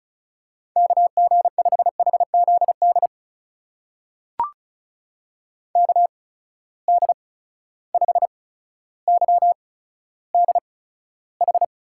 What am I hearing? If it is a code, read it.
Morse code, heard as a keyed tone, about 36 wpm, spaced as for 15 wpm, 700 Hz. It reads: KG5HZD KD5YDH
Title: USA Call Signs - 6 Characters - 8x Character Spacing - 35wpm